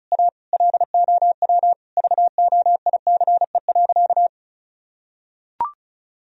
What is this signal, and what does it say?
Morse code, heard as a keyed tone, about 36 wpm, 700 Hz. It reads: A LOW VOICE.
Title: Sentences from Top 1000 Words - Review Part 1 - 35wpm